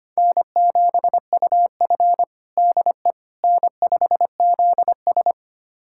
Morse code, heard as a keyed tone, about 25 wpm, 700 Hz. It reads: N7UF DE N5ZS
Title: USA Callsign DE Callsign - 4 Characters - 25wpm